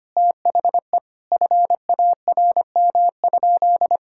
Morse code, heard as a keyed tone, about 25 wpm, 700 Hz. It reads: THE FARM?
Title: Sentences from Top 1000 Words - Review Part 6 - 25wpm